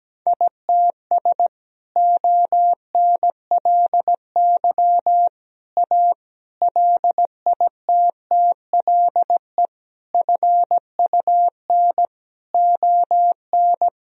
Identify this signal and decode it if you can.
Morse code, heard as a keyed tone, about 17 wpm, 700 Hz. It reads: ITS ONLY A LITTLE FUN ON
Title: Sentences from Top 1000 Words - Review Part 4 - 17wpm